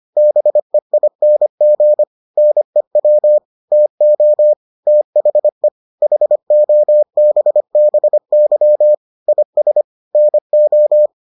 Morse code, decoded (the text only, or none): BEING NEW TO THE HOBBY IS NO